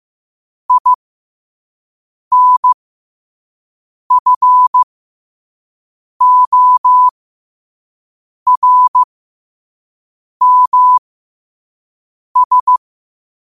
Morse code, heard as a keyed tone, about 15 wpm, spaced as for 5 wpm, 1000 Hz. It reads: INFORMS
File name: qotdfiles / 5wpm